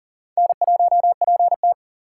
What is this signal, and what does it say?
Morse code, heard as a keyed tone, about 40 wpm, 700 Hz. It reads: N1PT
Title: USA Call Signs - 4 Characters - 3x Character Spacing - 40wpm